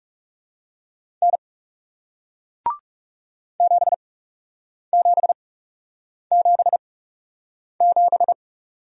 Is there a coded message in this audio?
Yes: Morse code from 1.2 to 8.3 s, heard as a keyed tone, about 40 wpm, spaced as for 8 wpm, 700 Hz. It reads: N 7777